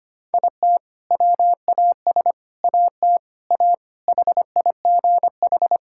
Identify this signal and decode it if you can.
Morse code, heard as a keyed tone, about 25 wpm, 700 Hz. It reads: IT WAS AT A HIGH
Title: Sentences from Top 1000 Words - Review Part 5 - 25wpm